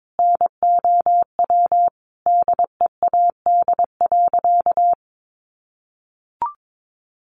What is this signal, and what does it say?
Morse code, heard as a keyed tone, about 22 wpm, 700 Hz. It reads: NOW DEAD.
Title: Sentences from Top 1500 Words 22wpm